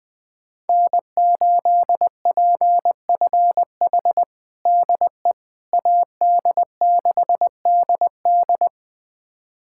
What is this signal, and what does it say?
Morse code, heard as a keyed tone, about 21 wpm, 700 Hz. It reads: N8PFH DE AD6DD
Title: USA Callsign DE Callsign - 5 Characters - 20wpm